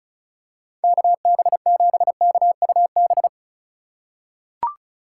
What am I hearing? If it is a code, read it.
Morse code, heard as a keyed tone, about 34 wpm, 700 Hz. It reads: KB7KUB
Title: USA Call Signs - 6 Characters - 4x Character Spacing - 35wpm